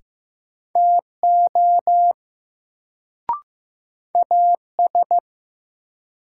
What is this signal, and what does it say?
Morse code, heard as a keyed tone, about 15 wpm, 700 Hz. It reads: TO AS
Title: Sets of 2 Words - New Word Drill - 15wpm